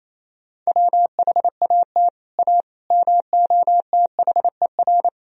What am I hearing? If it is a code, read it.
Morse code, heard as a keyed tone, about 28 wpm, 700 Hz. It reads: WHAT A MOTHER
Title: Sentences from Top 1000 Words - Review Part 1 - 28wpm